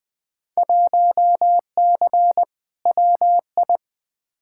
Morse code, heard as a keyed tone, about 20 wpm, 700 Hz. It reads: 1C WI
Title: Field Day Exchanges - 20wpm